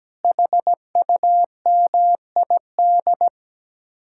Morse code, heard as a keyed tone, about 17 wpm, 700 Hz. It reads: HUMID
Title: Speed-Racing Common QSO Phrases Encore - 17wpm